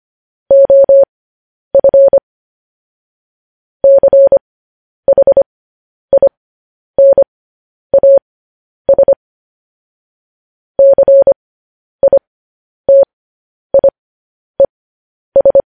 Morse code, heard as a keyed tone, about 25 wpm, spaced as for 10 wpm, 550 Hz. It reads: OF CHINAS CITIES
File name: podcast / News headlines-1x-25x10